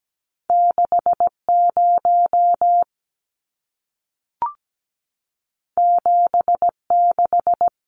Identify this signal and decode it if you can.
Morse code, heard as a keyed tone, about 17 wpm, 700 Hz. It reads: 60 76
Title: Numbers - 2-Digit 17wpm